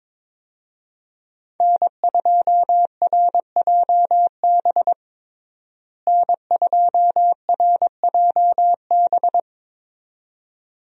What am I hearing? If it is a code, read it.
Morse code, heard as a keyed tone, about 22 wpm, 700 Hz. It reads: N2RJB N2RJB
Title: USA Call Signs - 5 Characters - Repeated 2x - 22wpm